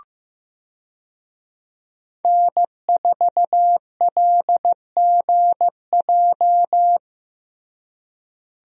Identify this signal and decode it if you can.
Morse code, heard as a keyed tone, about 15 wpm, 700 Hz. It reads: N4LGJ
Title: USA Call Signs - 5 Characters - Repeated 3x - 15wpm